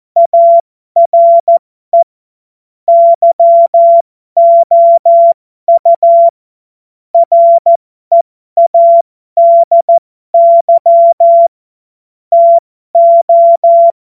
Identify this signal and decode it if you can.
Morse code, heard as a keyed tone, about 14 wpm, spaced as for 12 wpm, 680 Hz. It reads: ARE YOU READY TO
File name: Morsekurs / Witze